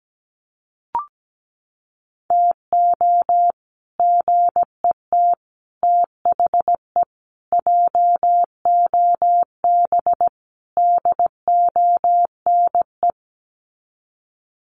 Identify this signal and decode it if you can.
Morse code, heard as a keyed tone, about 17 wpm, 700 Hz. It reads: TO GET THE JOB DONE
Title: Sets of 5 Words Encore2 17wpm